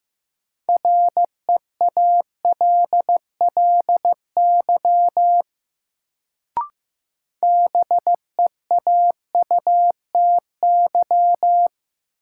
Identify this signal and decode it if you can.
Morse code, heard as a keyed tone, about 16 wpm, 700 Hz. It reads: REALLY BEAUTY
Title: Sentences from Top 600 Words - New Word Drill - 15wpm